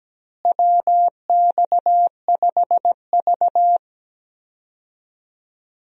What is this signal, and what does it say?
Morse code, heard as a keyed tone, about 17 wpm, 700 Hz. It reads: WX5V
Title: USA Call Signs - 4 Characters - Repeated 2x - 17wpm